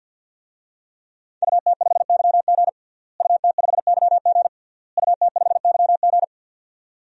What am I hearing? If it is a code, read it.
Morse code, heard as a keyed tone, about 48 wpm, 700 Hz. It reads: UT5YG UT5YG UT5YG
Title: DXCC-100 Call Signs - 5 Characters - Repeated 3x - 50wpm